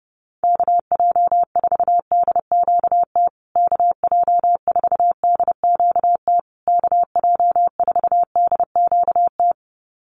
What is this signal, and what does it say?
Morse code, heard as a keyed tone, about 30 wpm, 700 Hz. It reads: KJ4DQT KJ4DQT KJ4DQT
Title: USA Call Signs Repeated 3x 30wpm